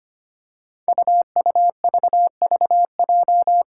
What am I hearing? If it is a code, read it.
Morse code, heard as a keyed tone, about 25 wpm, 700 Hz. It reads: UUVVJ
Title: Five Challenging Letter-Number Combinations 25wpm